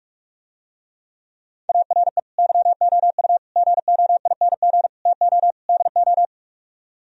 Morse code, heard as a keyed tone, about 48 wpm, 700 Hz. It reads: ARE YOU GOING TO DO